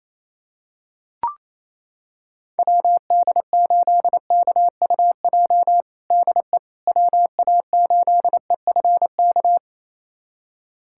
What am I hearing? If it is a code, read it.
Morse code, heard as a keyed tone, about 27 wpm, 700 Hz. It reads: WD8KUJ DE WA8EFK
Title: USA Callsign DE Callsign 28wpm